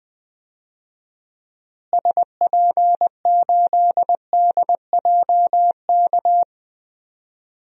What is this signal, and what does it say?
Morse code, heard as a keyed tone, about 20 wpm, 700 Hz. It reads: SP8DJK
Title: DXCC-100 Call Signs - 6 Characters - Repeated 3x - 20wpm